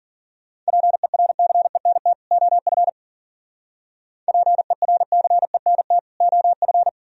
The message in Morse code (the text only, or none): PERCENT OF PERCENT OF